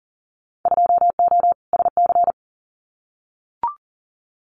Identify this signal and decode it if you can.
Morse code, heard as a keyed tone, about 42 wpm, 700 Hz. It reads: 2O SC
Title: Winter Field Day Exchanges 40wpm